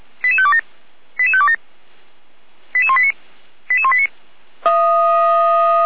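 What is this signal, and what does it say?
Five-tone selective call (ZVEI), heard as quick runs of five tones, tones of about 70 ms. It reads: 88527 88527 88189 88189
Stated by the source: ZVEI five-tone selective call 88355 received by SDR.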